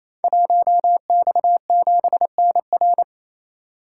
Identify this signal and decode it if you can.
Morse code, heard as a keyed tone, about 27 wpm, 700 Hz. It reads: 1X7NR
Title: Code Groups - 5 Letter:Numbers - 28wpm